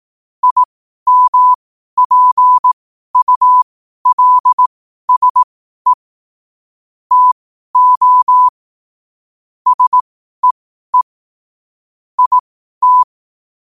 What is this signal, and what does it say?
Morse code, heard as a keyed tone, about 18 wpm, spaced as for 12 wpm, 1000 Hz. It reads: IMPULSE TO SEE IT
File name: qotdfiles / 12wpm